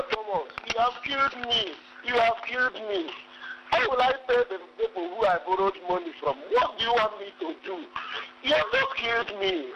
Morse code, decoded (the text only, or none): none